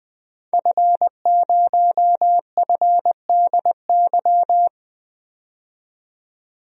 Morse code, heard as a keyed tone, about 20 wpm, 700 Hz. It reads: F0FDY
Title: DXCC-100 Call Signs - 5 Characters - 20wpm